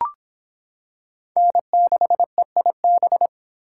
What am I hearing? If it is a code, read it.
Morse code, heard as a keyed tone, about 25 wpm, 700 Hz. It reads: N6EIB